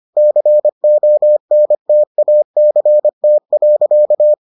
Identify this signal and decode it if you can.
Morse code, heard as a keyed tone, about 25 wpm, 600 Hz. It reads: CONTACT.